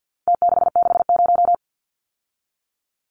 Morse code, heard as a keyed tone, about 50 wpm, 700 Hz. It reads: T660
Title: Numbers - Contest Style - 4-Digit 50wpm